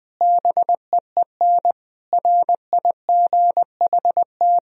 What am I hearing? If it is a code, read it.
Morse code, heard as a keyed tone, about 20 wpm, 700 Hz. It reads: BEEN RIGHT
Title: Sentences from Top 1000 Words - Review Part 4 - 20wpm